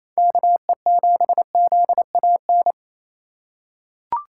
Morse code, heard as a keyed tone, about 27 wpm, 700 Hz. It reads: KE7ZAN
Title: USA Call Signs - 6 Characters - 6x Character Spacing - 28wpm